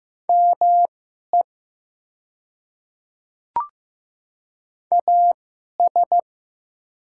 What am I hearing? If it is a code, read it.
Morse code, heard as a keyed tone, about 15 wpm, spaced as for 11 wpm, 700 Hz. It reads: ME AS
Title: 2x Character Spacing - Top 200 Words - 15wpm